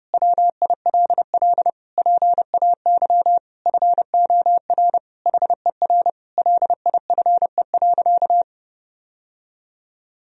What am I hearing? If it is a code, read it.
Morse code, heard as a keyed tone, about 32 wpm, 700 Hz. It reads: WILL PAY FOR HER LIFE.
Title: Sentences from Top 1000 Words - Review Part 4 - 30wpm